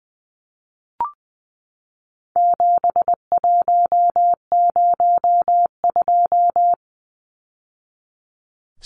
Morse code, heard as a keyed tone, about 20 wpm, 700 Hz. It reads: 7102
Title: Numbers - 4-Digit 20wpm